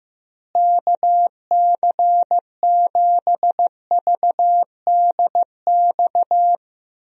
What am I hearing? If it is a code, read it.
Morse code, heard as a keyed tone, about 15 wpm, 700 Hz. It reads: KC7VDX
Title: USA Call Signs - 6 Characters - Repeated 3x - 15wpm